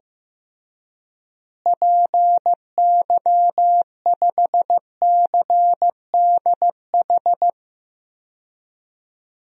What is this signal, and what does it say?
Morse code, heard as a keyed tone, about 15 wpm, 700 Hz. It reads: PY5CDH
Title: DXCC-100 Call Signs - 6 Characters - Repeated 3x - 15wpm